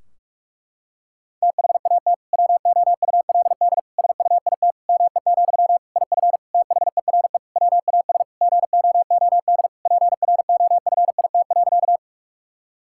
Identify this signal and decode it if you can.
Morse code, heard as a keyed tone, about 42 wpm, 700 Hz. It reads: THAT WOULD SUIT ME, IF THERE WAS GOOD PROFIT.
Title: Sentences from Top 1600 Words 45wpm